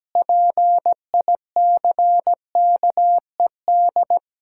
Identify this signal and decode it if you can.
Morse code, heard as a keyed tone, about 17 wpm, 700 Hz. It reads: PICKED
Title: Pangram Challenge v2 - 17wpm